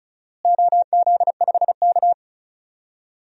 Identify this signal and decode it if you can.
Morse code, heard as a keyed tone, about 36 wpm, 700 Hz. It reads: OZ5K
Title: DXCC-100 Call Signs - 4 Characters - Repeated 3x - 35wpm